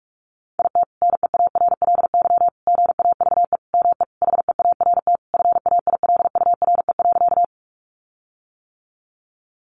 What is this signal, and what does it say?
Morse code, heard as a keyed tone, about 44 wpm, 700 Hz. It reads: IT NEARLY GAVE ME HEART FAILURE.
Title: Sentences from Top 1300 Words 45wpm